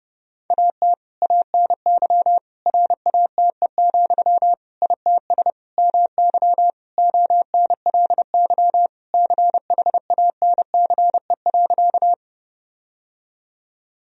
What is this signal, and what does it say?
Morse code, heard as a keyed tone, about 32 wpm, 700 Hz. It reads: AT ANY RATE, ITS MY ONLY CHANCE.
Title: Sentences from Top 1000 Words - Review Part 4 - 30wpm